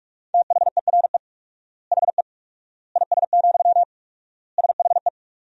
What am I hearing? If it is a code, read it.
Morse code, heard as a keyed tone, about 46 wpm, 700 Hz. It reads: THERE HE IS, SHE
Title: 4x Word-Spacing - Sentences from Top 100 Words - 45wpm